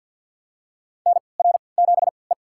Morse code, heard as a keyed tone, about 46 wpm, spaced as for 28 wpm, 700 Hz. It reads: NR7E
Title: USA Call Signs - 4 Characters - 3x Character Spacing - 50wpm